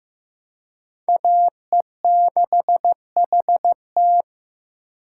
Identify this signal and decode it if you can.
Morse code, heard as a keyed tone, about 15 wpm, 700 Hz. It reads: AE6HT
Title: USA Call Signs - 2x Character Spacing - 15wpm